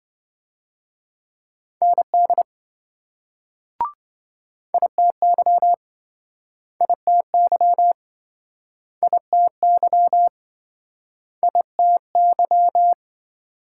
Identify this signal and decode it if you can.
Morse code, heard as a keyed tone, about 26 wpm, 700 Hz. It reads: ND ITY ITY ITY ITY